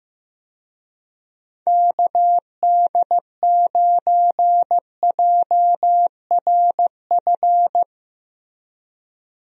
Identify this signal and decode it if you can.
Morse code, heard as a keyed tone, about 15 wpm, 700 Hz. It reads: KD9JRF
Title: USA Call Signs - 6 Characters - Repeated 3x - 15wpm